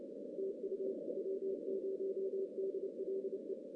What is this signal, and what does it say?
Morse code, heard as a keyed tone, about 29 wpm, 380 Hz, with noise. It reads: TU W0DLE
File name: 2016 JAN CW NAQP / 2016 Jan naqp k5ot - 0-9